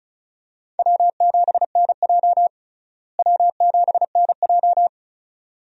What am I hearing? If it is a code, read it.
Morse code, heard as a keyed tone, about 36 wpm, 700 Hz. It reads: W7NJ W7NJ
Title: USA Call Signs - 4 Characters - Repeated 2x - 35wpm